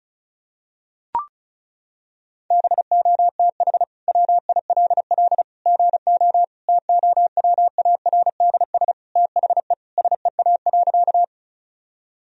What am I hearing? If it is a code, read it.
Morse code, heard as a keyed tone, about 35 wpm, 700 Hz. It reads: BOTH WILL GO TOWARDS THE SEA.
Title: Sentences from Top 1000 Words - Review Part 2 - 35wpm